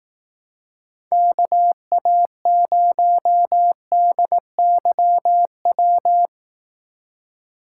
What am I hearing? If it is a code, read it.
Morse code, heard as a keyed tone, about 18 wpm, 700 Hz. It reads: KA0DYW
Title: Speed Racing - USA Call Signs 15wpm